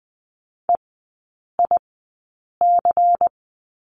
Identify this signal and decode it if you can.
Morse code, heard as a keyed tone, about 20 wpm, spaced as for 8 wpm, 700 Hz. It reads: EIC
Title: 093 Lesson 16e - Letter c - Three Characters 20wpm